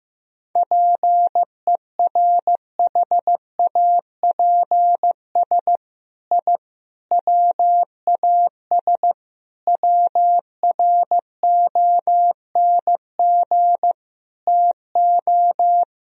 Morse code, heard as a keyed tone, about 15 wpm, 700 Hz. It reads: PERHAPS I WAS WRONG TO
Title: Sentences from Top 1000 Words - Review Part 1 - 15wpm